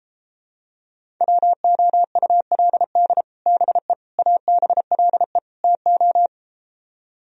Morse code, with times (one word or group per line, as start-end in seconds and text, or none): 1.2-3.2 WOULD
3.5-3.9 BE
4.2-5.4 ABLE
5.6-6.3 TO